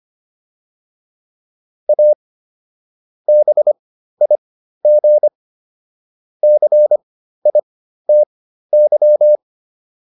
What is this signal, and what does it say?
Morse code, heard as a keyed tone, about 25 wpm, spaced as for 13 wpm, 600 Hz. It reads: A BIG CITY